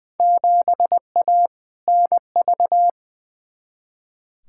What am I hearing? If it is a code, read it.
Morse code, heard as a keyed tone, about 21 wpm, 700 Hz. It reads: 7A NV
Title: Field Day Exchanges Encore - 20wpm